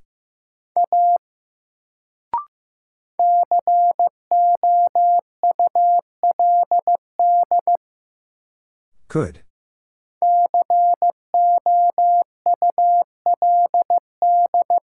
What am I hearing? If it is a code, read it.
Morse code, heard as a keyed tone, about 15 wpm, 700 Hz. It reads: A COULD COULD
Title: Sentences from Top 100 Words - New Word Drill - 15wpm